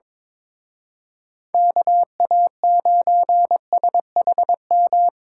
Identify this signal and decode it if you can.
Morse code, heard as a keyed tone, about 25 wpm, 700 Hz. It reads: KA9SHM
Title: Speed Racing - USA 2x3 Call Signs 22wpm